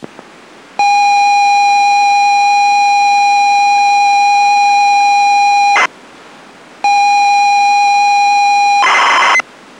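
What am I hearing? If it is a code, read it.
ZX Spectrum tape loading.
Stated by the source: ZX Spectrum tape loading from the game Deep Zone, dumped from the original cassette.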